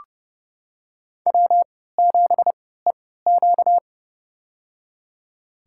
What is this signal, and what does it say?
Morse code, heard as a keyed tone, about 32 wpm, spaced as for 17 wpm, 700 Hz. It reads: W7EQ
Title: USA Call Signs - 4 Characters - 3x Character Spacing - 30wpm